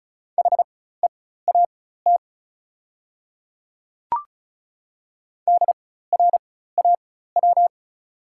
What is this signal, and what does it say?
Morse code, heard as a keyed tone, about 34 wpm, spaced as for 16 wpm, 690 Hz. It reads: HEAT DRAW
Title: 4x Character Spacing - Top 500 Words - 35wpm